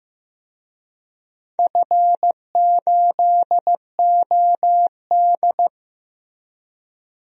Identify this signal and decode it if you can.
Morse code, heard as a keyed tone, about 15 wpm, 700 Hz. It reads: F8OD
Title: DXCC-100 Call Signs - 4 Characters - Repeated 2x - 15wpm